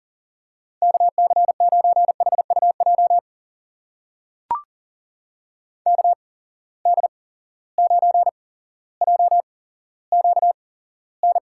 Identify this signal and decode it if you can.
Morse code, heard as a keyed tone, about 40 wpm, spaced as for 17 wpm, 700 Hz. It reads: KC9HUJ KD9JQN
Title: USA Call Signs - 6 Characters - 8x Character Spacing - 40wpm